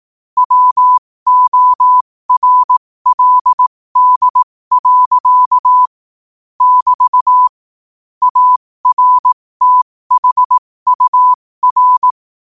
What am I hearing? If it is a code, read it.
Morse code, heard as a keyed tone, about 18 wpm, spaced as for 15 wpm, 1000 Hz. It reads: WORLD. <BT> ARTHUR
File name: 15wpm